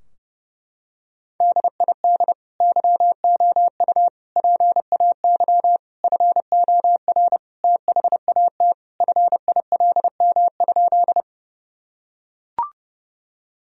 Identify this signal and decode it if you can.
Morse code, heard as a keyed tone, about 30 wpm, 700 Hz. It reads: DID YOU PAY FOR THAT FILM?
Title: Sentences from Top 1000 Words - Review Part 3 - 30wpm